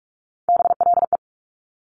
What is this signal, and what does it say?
Morse code, heard as a keyed tone, about 46 wpm, 690 Hz. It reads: BLE
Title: Top Three Letter Combinations in Words 45wpm